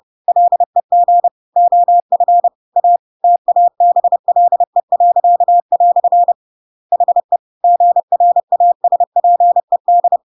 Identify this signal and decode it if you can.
Morse code, heard as a keyed tone, about 30 wpm, 700 Hz. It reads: LEG OF A TABLE." HE GRASPED
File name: Pinocchio / Pinocchio ch1 30 30 ik1hge